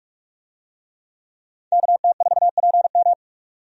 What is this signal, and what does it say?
Morse code, heard as a keyed tone, about 46 wpm, 700 Hz. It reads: KT4PM
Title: USA Call Signs - 5 Characters - 45wpm